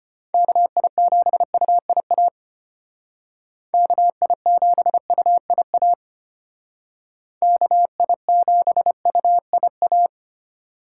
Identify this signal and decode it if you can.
Morse code, heard as a keyed tone, about 32 wpm, 700 Hz. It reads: KI7UIA KI7UIA KI7UIA